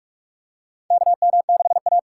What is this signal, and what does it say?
Morse code, heard as a keyed tone, about 46 wpm, 700 Hz. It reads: KM6A
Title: USA Call Signs - 4 Characters - 6x Character Spacing - 45wpm